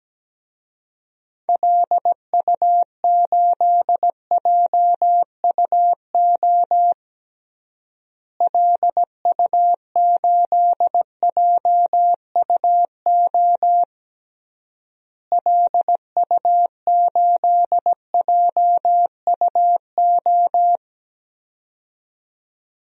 Morse code, heard as a keyed tone, about 17 wpm, 700 Hz. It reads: LU8JUO LU8JUO LU8JUO
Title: DXCC-100 Call Signs - 6 Characters - Repeated 3x - 17wpm